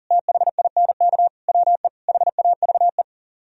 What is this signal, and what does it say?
Morse code, heard as a keyed tone, about 51 wpm, 700 Hz. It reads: THINK WE HAVE